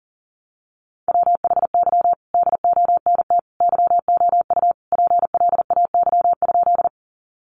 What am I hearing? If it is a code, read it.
Morse code, heard as a keyed tone, about 40 wpm, 700 Hz. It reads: WHY DONT YOU PLAY?
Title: Sentences from Top 300 Words 40wpm